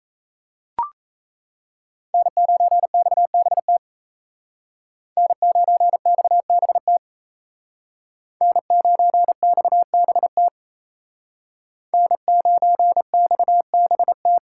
Morse code, heard as a keyed tone, about 35 wpm, 700 Hz. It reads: N9XBT N9XBT N9XBT N9XBT
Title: Speed Racing - USA 1x3 Call Signs 28wpm